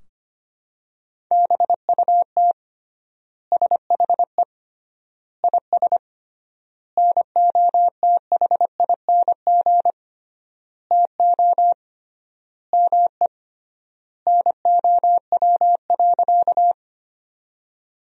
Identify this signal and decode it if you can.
Morse code, heard as a keyed tone, about 25 wpm, 700 Hz. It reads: BUT SHE IS NOTHING TO ME NOW.
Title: 3x Word-Spacing - Sentences from Top 500 Words - 25wpm